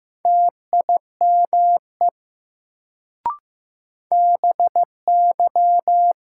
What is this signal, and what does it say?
Morse code, heard as a keyed tone, about 15 wpm, 700 Hz. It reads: TIME BY
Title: Sentences from Top 100 Words - New Word Drill - 15wpm